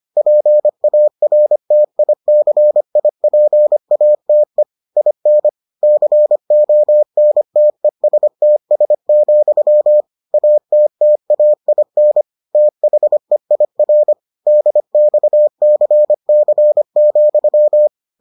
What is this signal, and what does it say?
Morse code, heard as a keyed tone, about 25 wpm, 600 Hz. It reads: PARTICIPATE IN CONTESTS, ATTAIN THEIR DXCC,